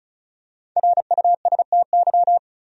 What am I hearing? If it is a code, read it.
Morse code, heard as a keyed tone, about 34 wpm, 700 Hz. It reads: RUSTY
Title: USA Common Names - Max 5 Letters - 35wpm